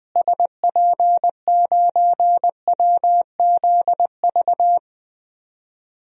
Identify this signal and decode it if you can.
Morse code, heard as a keyed tone, about 21 wpm, 700 Hz. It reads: SP9WZV
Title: DXCC-100 Call Signs - 6 Characters - Repeated 2x - 20wpm